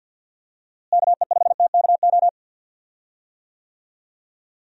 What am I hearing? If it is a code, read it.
Morse code, heard as a keyed tone, about 51 wpm, 700 Hz. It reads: KE5TKO
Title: USA Call Signs - 6 Characters - Repeated 2x - 50wpm